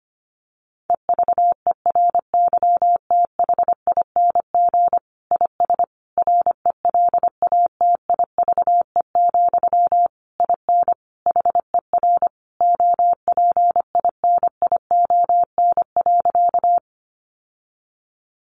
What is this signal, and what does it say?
Morse code, heard as a keyed tone, about 25 wpm, 700 Hz. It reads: EVERYTHING IS RELATIVE, IN HER OPINION.
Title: Sentences from Top 1400 Words 25wpm